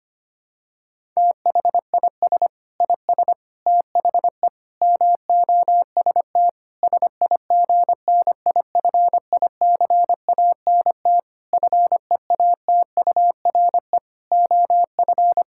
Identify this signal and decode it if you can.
Morse code, heard as a keyed tone, about 25 wpm, 700 Hz. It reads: THIS IS THE MOST SIGNIFICANT FEATURE OF
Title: Sentences from Top 1000 Words - Review Part 6 - 25wpm